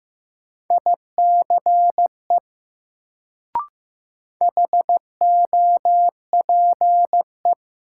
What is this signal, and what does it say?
Morse code, heard as a keyed tone, about 15 wpm, 700 Hz. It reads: ICE HOPE
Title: Sentences from Top 1000 Words - New Word Drill - 15wpm